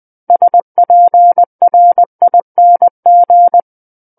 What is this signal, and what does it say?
Morse code, heard as a keyed tone, about 20 wpm, 700 Hz. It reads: SPRING
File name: 20 WPM